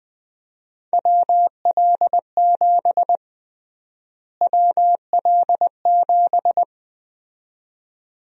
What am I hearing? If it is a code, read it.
Morse code, heard as a keyed tone, about 20 wpm, 700 Hz. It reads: WL7 WL7
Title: USA Call Sign Prefixes - Repeated 2x - 20wpm